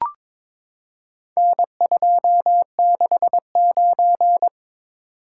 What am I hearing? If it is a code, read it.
Morse code, heard as a keyed tone, about 22 wpm, 700 Hz. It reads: N269
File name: practice / 22wpm / Four Letter-Number Combinations - 22wpm